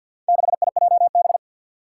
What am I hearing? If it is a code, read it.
Morse code, heard as a keyed tone, about 46 wpm, 700 Hz. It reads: 6IJB
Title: Four Challenging Letter-Number Combinations 50wpm